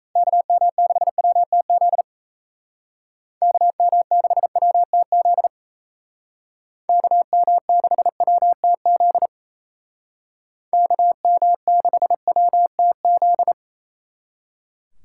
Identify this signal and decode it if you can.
Morse code, heard as a keyed tone, about 34 wpm, 700 Hz. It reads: KM6WTZ KM6WTZ KM6WTZ KM6WTZ
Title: Speed Racing - USA Call Signs 28wpm